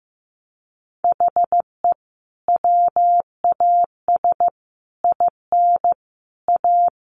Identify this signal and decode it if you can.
Morse code, heard as a keyed tone, about 15 wpm, 700 Hz. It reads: HE WAS IN A
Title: Sentences from Top 1100 Words 15wpm